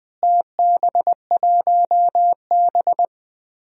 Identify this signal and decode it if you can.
Morse code, heard as a keyed tone, about 20 wpm, 700 Hz. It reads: TB1B
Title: DXCC-100 Call Signs - 4 Characters - Repeated 2x - 20wpm